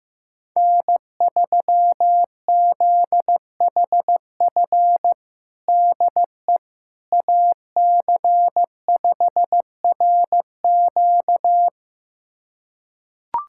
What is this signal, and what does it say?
Morse code, heard as a keyed tone, about 15 wpm, 700 Hz. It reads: N3ZHF DE AC5RQ
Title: USA Callsign DE Callsign - 5 Characters - 15wpm